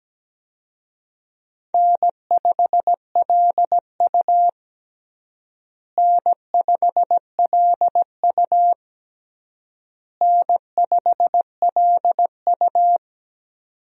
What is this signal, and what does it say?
Morse code, heard as a keyed tone, about 17 wpm, 700 Hz. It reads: N5LU N5LU N5LU
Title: USA Call Signs - 4 Characters - Repeated 3x - 17wpm